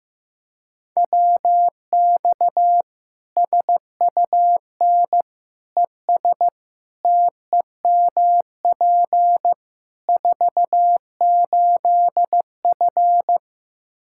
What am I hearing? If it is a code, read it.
Morse code, heard as a keyed tone, about 15 wpm, 700 Hz. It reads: WX SUN ES TEMP 48F
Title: K1USN SST QSO Practice 15wpm